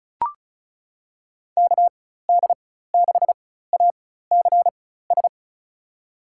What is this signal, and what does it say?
Morse code, heard as a keyed tone, about 36 wpm, spaced as for 16 wpm, 700 Hz. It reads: KD6ACS
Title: USA Call Signs - 4x Character Spacing - 35wpm